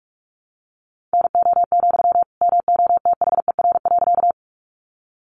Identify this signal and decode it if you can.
Morse code, heard as a keyed tone, about 44 wpm, 700 Hz. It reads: NO, MOTHER.
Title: Sentences from Top 200 Words 45wpm